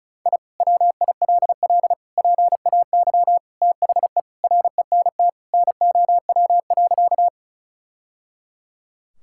Morse code, heard as a keyed tone, about 36 wpm, 700 Hz. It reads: I WILL PAY THE RENT NOW.
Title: Sentences from Top 1000 Words - Review Part 5 - 35wpm